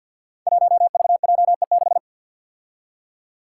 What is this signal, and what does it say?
Morse code, heard as a keyed tone, about 52 wpm, 700 Hz. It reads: 1VJE6
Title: Five Challenging Letter-Number Combinations 50wpm